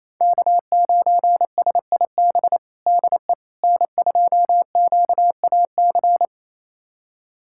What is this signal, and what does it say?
Morse code, heard as a keyed tone, about 28 wpm, 700 Hz. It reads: K9SIB DE N2QAC
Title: USA Callsign DE Callsign - 5 Characters - 28wpm